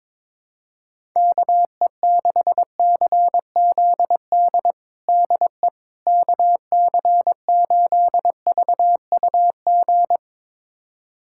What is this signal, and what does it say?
Morse code, heard as a keyed tone, about 22 wpm, 700 Hz. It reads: KE6CZD DE KC8VUG
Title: USA Callsign DE Callsign - 6 Characters - 22wpm